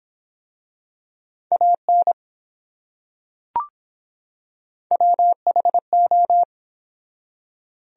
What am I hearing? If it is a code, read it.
Morse code, heard as a keyed tone, about 24 wpm, 700 Hz. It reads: AN WHO